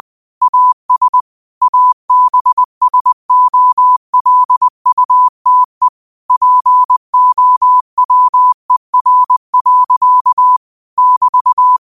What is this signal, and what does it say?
Morse code, heard as a keyed tone, about 20 wpm, 1000 Hz. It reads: AS ABSOLUTE POWER. <BT>
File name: qotdfiles / 20wpm